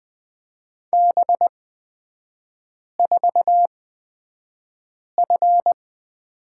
Morse code, heard as a keyed tone, about 20 wpm, spaced as for 5 wpm, 700 Hz. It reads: B4F
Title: 243 Lesson 39e - Letter z - Three Characters 20wpm